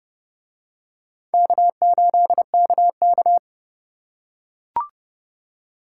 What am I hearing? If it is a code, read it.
Morse code, heard as a keyed tone, about 28 wpm, 700 Hz. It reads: K8KK